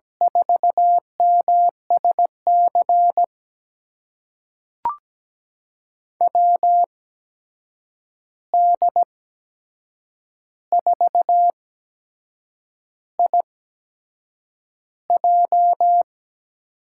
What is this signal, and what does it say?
Morse code, heard as a keyed tone, about 17 wpm, spaced as for 7 wpm, 700 Hz. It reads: 4MSC WD4IJ
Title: USA Call Signs - 8x Character Spacing - 17wpm